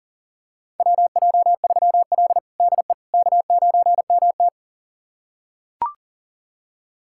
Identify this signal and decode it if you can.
Morse code, heard as a keyed tone, about 40 wpm, 700 Hz. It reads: WJ3L DE K9MT
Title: USA Callsign DE Callsign - 4 Characters - 40wpm